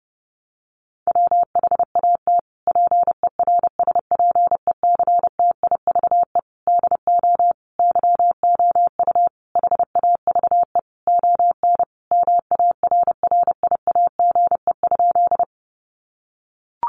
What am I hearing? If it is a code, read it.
Morse code, heard as a keyed tone, about 30 wpm, 700 Hz. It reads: WHAT PERSPECTIVE DO YOU HAVE ON MARRIAGE?
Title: Sentences from Top 1200 Words 30wpm